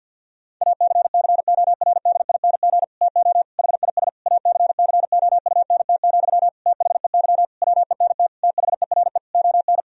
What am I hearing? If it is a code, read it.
Morse code, heard as a keyed tone, about 51 wpm, 700 Hz. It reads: ACCORDING TO HIS ACCOUNT, THEY WENT THERE ON
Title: Sentences from Top 1000 Words - Review Part 4 - 50wpm